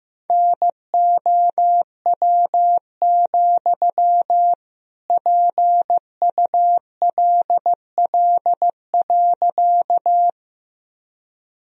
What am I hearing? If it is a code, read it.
Morse code, heard as a keyed tone, about 16 wpm, 700 Hz. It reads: NOW, PULL.
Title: Sentences from Top 1000 Words - Review Part 2 - 15wpm